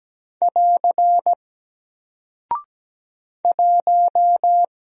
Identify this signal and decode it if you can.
Morse code, heard as a keyed tone, about 17 wpm, 700 Hz. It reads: <AR> 1
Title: Single Character (Letter:Number:Punctuation:Prosign) 17wpm